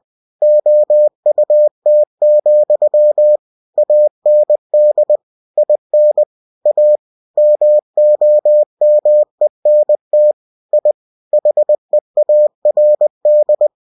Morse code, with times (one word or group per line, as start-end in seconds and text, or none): 0.4-3.4 OUT,
3.8-5.2 AND
5.6-6.2 IN
6.7-7.0 A
7.4-10.3 MOMENT
10.7-10.9 I
11.3-13.7 HEARD